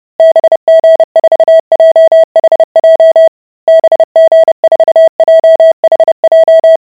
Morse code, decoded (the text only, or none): BG4JHJ BG4JHJ